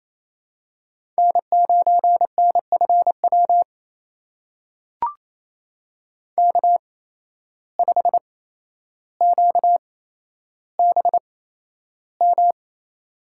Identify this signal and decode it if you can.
Morse code, heard as a keyed tone, about 28 wpm, spaced as for 12 wpm, 700 Hz. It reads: N9NFW K5QBM
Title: USA Call Signs - 5 Characters - 8x Character Spacing - 28wpm